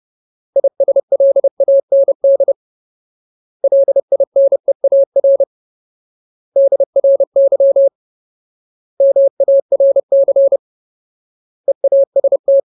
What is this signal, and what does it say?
Morse code, heard as a keyed tone, about 30 wpm, 550 Hz. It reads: ISLAND LINEAR DRY MARC EAST
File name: Ultimate Word List / 30WPM Ultimate